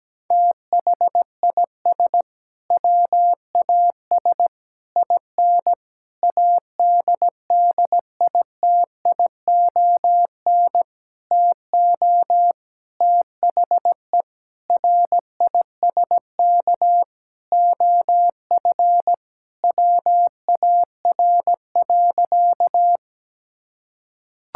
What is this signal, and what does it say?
Morse code, heard as a keyed tone, about 17 wpm, 700 Hz. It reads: THIS WAS IN ADDITION TO THE RISK OF WAR.
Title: Sentences from Top 1000 Words - Review Part 6 - 17wpm